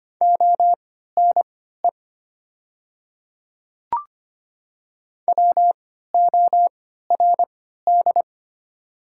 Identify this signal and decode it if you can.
Morse code, heard as a keyed tone, about 25 wpm, spaced as for 14 wpm, 700 Hz. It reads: ONE WORD
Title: 3x Character Spacing - Top 200 Words - 25wpm